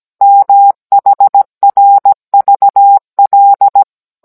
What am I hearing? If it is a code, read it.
Morse code, heard as a keyed tone, about 17 wpm, 800 Hz. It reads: MHRVL